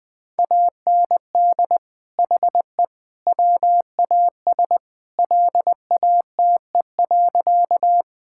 Morse code, heard as a keyed tone, about 20 wpm, 700 Hz. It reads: AND HE WAS LATE.
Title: Sentences with 3-Word Phrases - Spoken Beginning - 20wpm